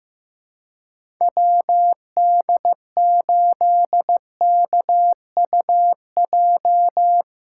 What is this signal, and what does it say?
Morse code, heard as a keyed tone, about 16 wpm, 700 Hz. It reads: WD8KUJ
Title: USA Callsign DE Callsign 15wpm